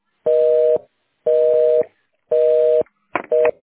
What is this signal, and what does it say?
Busy tone (North America), 480 + 620 Hz, with noise.
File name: 1040.3000.5ab20274-f6fc-4abf-bdae